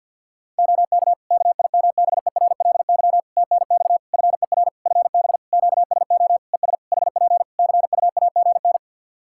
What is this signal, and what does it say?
Morse code, heard as a keyed tone, about 49 wpm, 700 Hz. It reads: OK KIMBERLY TNX FER FB QSO ES HP CUAGN
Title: Practice QSO - Exchange 4 - Ending - 50wpm